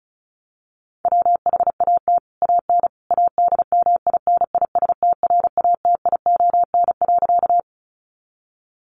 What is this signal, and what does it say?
Morse code, heard as a keyed tone, about 34 wpm, 700 Hz. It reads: WHAT AN ADMINISTRATION.
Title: Sentences from Top 1300 Words 35wpm